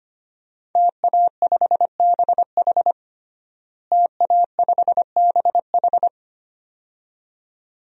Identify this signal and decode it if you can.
Morse code, heard as a keyed tone, about 25 wpm, 700 Hz. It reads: TA5BH TA5BH
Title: DXCC-100 Call Signs - 5 Characters - Repeated 2x - 25wpm